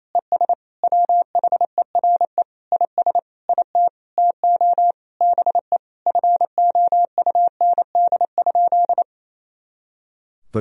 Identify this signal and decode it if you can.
Morse code, heard as a keyed tone, about 28 wpm, 700 Hz. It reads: ES WHERE IS IT TO BE FOUND?
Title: Sentences from Top 300 Words and Abbreviations 28wpm